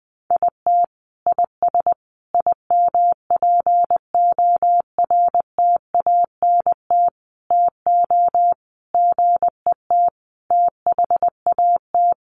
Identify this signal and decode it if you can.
Morse code, heard as a keyed tone, about 20 wpm, 700 Hz. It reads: IT IS IMPORTANT TO GET THAT
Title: Sentences from Top 1100 Words 20wpm